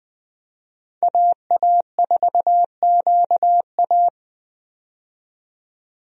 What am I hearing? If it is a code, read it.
Morse code, heard as a keyed tone, about 20 wpm, 700 Hz. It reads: AA4QA
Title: USA Call Signs - 5 Characters - 20wpm